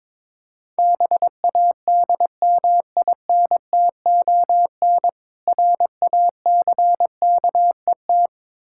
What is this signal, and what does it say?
Morse code, heard as a keyed tone, about 23 wpm, 690 Hz. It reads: BADMINTON RACKET
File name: Difficult Words with Hints - 22wpm